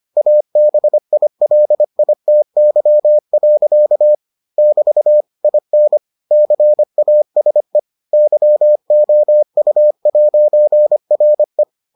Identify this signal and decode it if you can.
Morse code, heard as a keyed tone, about 25 wpm, 600 Hz. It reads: ABILITY. <BT> IN CASE YOU'RE